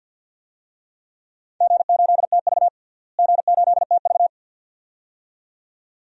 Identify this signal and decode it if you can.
Morse code, heard as a keyed tone, about 48 wpm, 700 Hz. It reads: G8TV G8TV
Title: DXCC-100 Call Signs - 4 Characters - Repeated 2x - 50wpm